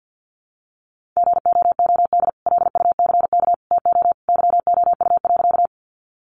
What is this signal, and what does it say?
Morse code, heard as a keyed tone, about 50 wpm, 700 Hz. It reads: GOOD LUCK TO YOU.
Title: Sentences from Top 1500 Words 50wpm